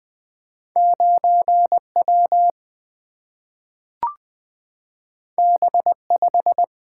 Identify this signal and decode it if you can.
Morse code, heard as a keyed tone, about 20 wpm, 700 Hz. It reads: 9 W B 5
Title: Single-Letter Single-Number 20wpm